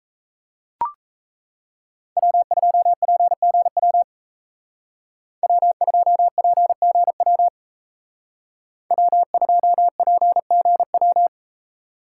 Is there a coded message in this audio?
Yes: Morse code from 2.2 to 11.3 s, heard as a keyed tone, about 36 wpm, 700 Hz. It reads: W2PGW W2PGW W2PGW